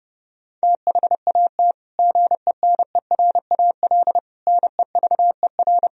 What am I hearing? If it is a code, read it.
Morse code, heard as a keyed tone, about 30 wpm, 700 Hz. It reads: THAT GENERAL NEVER
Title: Sentences from Top 1000 Words - Review Part 4 - 30wpm